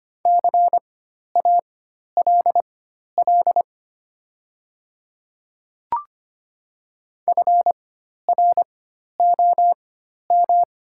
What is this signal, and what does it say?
Morse code, heard as a keyed tone, about 25 wpm, spaced as for 12 wpm, 690 Hz. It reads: CALL FROM
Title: 4x Character Spacing - Top 100 Words - 25wpm